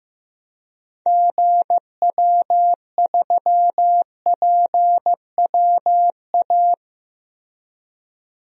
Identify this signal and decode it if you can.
Morse code, heard as a keyed tone, about 15 wpm, 700 Hz. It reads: GW3PWA
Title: DXCC-100 Call Signs - 6 Characters - Repeated 3x - 15wpm